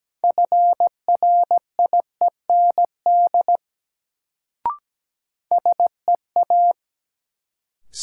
Morse code, heard as a keyed tone, about 17 wpm, 700 Hz. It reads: FRIEND SEA
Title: Sentences from Top 300 Words - New Word Drill - 17wpm